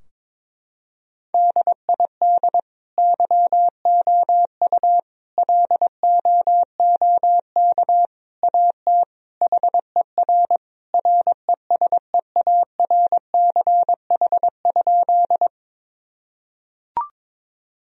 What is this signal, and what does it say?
Morse code, heard as a keyed tone, about 22 wpm, 700 Hz. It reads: DID YOU LOOK AT HER RESEARCH?
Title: Sentences from Top 1000 Words - Review Part 4 - 22wpm